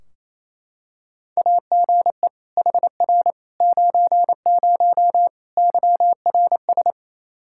Morse code, heard as a keyed tone, about 28 wpm, 700 Hz. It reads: AGE HR 90 YRS
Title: Practice QSO - Exchange 3 - More Info - 28wpm